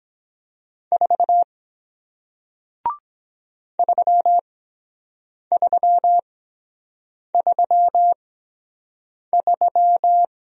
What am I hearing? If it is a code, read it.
Morse code, heard as a keyed tone, about 23 wpm, spaced as for 7 wpm, 700 Hz. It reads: 4 3333